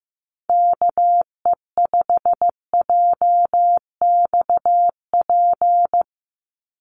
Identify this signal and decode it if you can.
Morse code, heard as a keyed tone, about 15 wpm, 700 Hz. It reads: KE5JXP
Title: USA Call Signs 15wpm